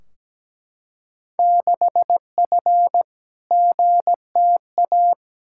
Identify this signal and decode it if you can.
Morse code, heard as a keyed tone, about 17 wpm, 700 Hz. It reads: 6F GTA
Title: Field Day Exchanges Encore - 17wpm